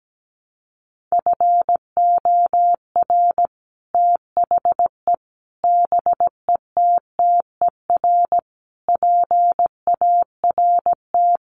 Morse code, heard as a keyed tone, about 17 wpm, 700 Hz. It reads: FOR THE BETTER PART
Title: Sets of 5 Words Encore2 17wpm